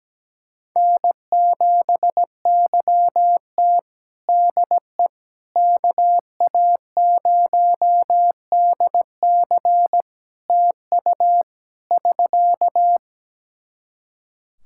Morse code, heard as a keyed tone, about 17 wpm, 700 Hz. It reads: N7YT DE KA0DC TU <SK>
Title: Practice QSO - Exchange 4 - Ending - 17wpm